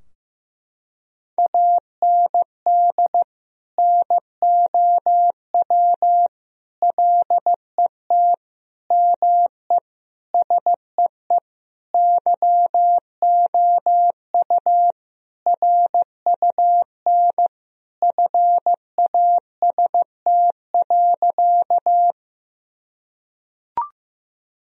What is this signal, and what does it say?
Morse code, heard as a keyed tone, about 15 wpm, 700 Hz. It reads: AND NOW LET ME SEE YOU RUN FAST.
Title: Sentences from Top 1000 Words - Review Part 2 - 15wpm